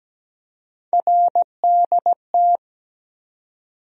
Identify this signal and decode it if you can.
Morse code, heard as a keyed tone, about 17 wpm, 700 Hz. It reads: RDT
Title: USA Call Sign Postfixes - 3 Letters 17wpm